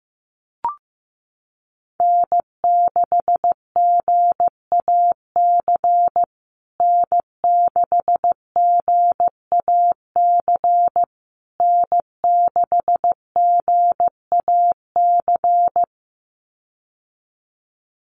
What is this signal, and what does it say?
Morse code, heard as a keyed tone, about 15 wpm, 700 Hz. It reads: N6GAC N6GAC N6GAC
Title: USA Call Signs Repeated 3x 15wpm